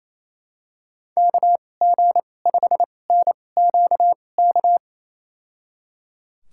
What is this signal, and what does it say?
Morse code, heard as a keyed tone, about 28 wpm, spaced as for 20 wpm, 700 Hz. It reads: KG5NQK
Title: USA Call Signs - 6 Characters - 2x Character Spacing - 28wpm